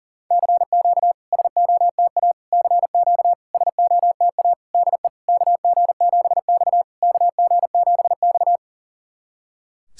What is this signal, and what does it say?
Morse code, heard as a keyed tone, about 39 wpm, 690 Hz. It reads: CQ SOTA CQ SOTA DE KG7X KG7X
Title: SOTA QSO Practice 40wpm